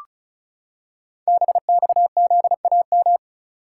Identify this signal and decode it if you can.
Morse code, heard as a keyed tone, about 36 wpm, 700 Hz. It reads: BXZAM
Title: Code Groups - 5 Letters - 35wpm